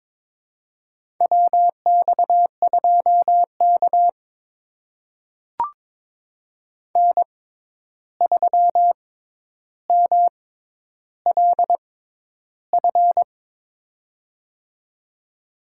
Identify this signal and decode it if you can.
Morse code, heard as a keyed tone, about 22 wpm, spaced as for 11 wpm, 700 Hz. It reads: WX2K N3MLF
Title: USA Call Signs - 6x Character Spacing - 22wpm